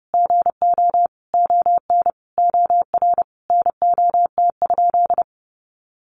Morse code, heard as a keyed tone, about 30 wpm, 700 Hz. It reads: GO ON OR NOT?
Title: Sentences from Top 600 Words 30wpm